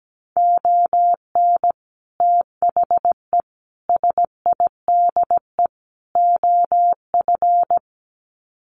Morse code, heard as a keyed tone, about 17 wpm, 700 Hz. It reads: ON THE SIDE OF
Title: Sets of 4 Words 17wpm